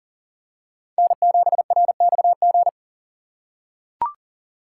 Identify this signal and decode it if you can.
Morse code, heard as a keyed tone, about 40 wpm, 700 Hz. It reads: N7RXG
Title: USA Call Signs - 3x Character Spacing - 40wpm